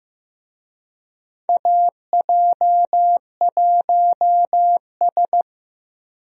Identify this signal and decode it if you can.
Morse code, heard as a keyed tone, about 15 wpm, 700 Hz. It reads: AJ1S
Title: USA Call Signs - 4 Characters - Repeated 2x - 15wpm